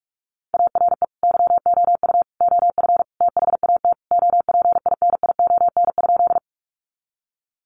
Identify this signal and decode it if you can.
Morse code, heard as a keyed tone, about 44 wpm, 700 Hz. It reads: ARE YOU OF THAT OPINION?
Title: Sentences from Top 1200 Words 45wpm